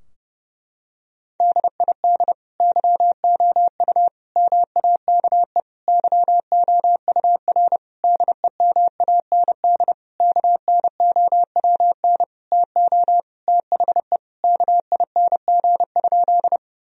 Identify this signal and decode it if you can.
Morse code, heard as a keyed tone, about 30 wpm, 700 Hz. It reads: DID YOU MAKE YOUR DEMAND KNOWN TO THE KING?
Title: Sentences from Top 1000 Words - Review Part 4 - 30wpm